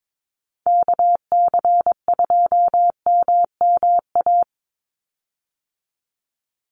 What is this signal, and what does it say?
Morse code, heard as a keyed tone, about 22 wpm, 700 Hz. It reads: KC2MMA
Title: USA Call Signs 22wpm